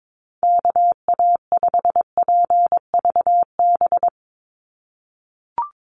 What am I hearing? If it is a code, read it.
Morse code, heard as a keyed tone, about 22 wpm, 700 Hz. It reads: KA5PVB
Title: NA SOTA Chaser Callsigns 22wpm